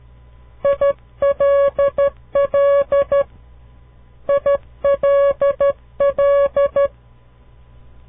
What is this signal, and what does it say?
Morse code, heard as a keyed tone, about 12 wpm, 560 Hz. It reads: ILL ILL
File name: CW Learning Programs / farnsworth / unit 2